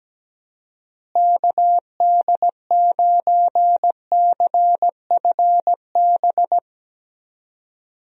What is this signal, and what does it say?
Morse code, heard as a keyed tone, about 17 wpm, 700 Hz. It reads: KD9CFB
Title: USA Call Signs - 6 Characters - Repeated 2x - 17wpm